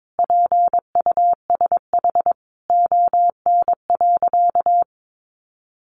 Morse code, heard as a keyed tone, about 22 wpm, 700 Hz. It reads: PUSH ON.
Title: Sentences from Top 1300 Words 22wpm